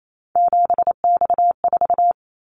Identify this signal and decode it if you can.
Morse code, heard as a keyed tone, about 28 wpm, 700 Hz. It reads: 7X4
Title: Callsign Prefixes for Top 100 DXCC - 28wpm